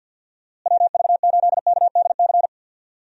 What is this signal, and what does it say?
Morse code, heard as a keyed tone, about 50 wpm, 700 Hz. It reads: WV8KDC
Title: USA Call Signs - 6 Characters - Repeated 2x - 50wpm